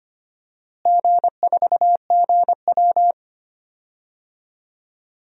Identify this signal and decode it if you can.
Morse code, heard as a keyed tone, about 25 wpm, 700 Hz. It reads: G4GW
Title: DXCC-100 Call Signs - 4 Characters - Repeated 3x - 25wpm